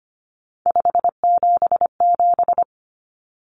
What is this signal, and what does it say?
Morse code, heard as a keyed tone, about 25 wpm, 700 Hz. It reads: 577
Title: Common Cut Numbers - 3-Digit 25wpm